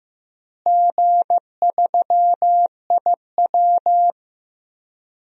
Morse code, heard as a keyed tone, about 15 wpm, 700 Hz. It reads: G3IW
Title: DXCC-100 Call Signs - 4 Characters - Repeated 3x - 15wpm